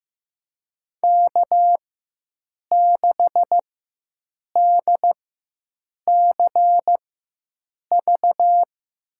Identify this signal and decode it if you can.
Morse code, heard as a keyed tone, about 15 wpm, spaced as for 7 wpm, 700 Hz. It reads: K6DCV
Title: USA Call Signs - 5 Characters - 4x Character Spacing - 15wpm